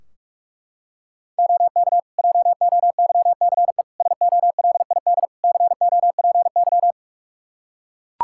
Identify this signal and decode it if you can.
Morse code, heard as a keyed tone, about 44 wpm, 700 Hz. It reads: OK JOYCE SOLID COPY
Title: Practice QSO - Exchange 3 - More Info - 45wpm